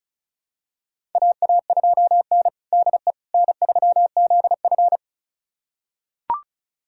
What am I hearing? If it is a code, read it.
Morse code, heard as a keyed tone, about 36 wpm, 700 Hz. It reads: AA2N DE N3ZF
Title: USA Callsign DE Callsign - 4 Characters - 35wpm